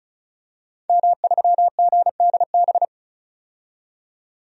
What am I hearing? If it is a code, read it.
Morse code, heard as a keyed tone, about 36 wpm, 700 Hz. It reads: M3GDB
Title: DXCC-100 Call Signs - 5 Characters - 35wpm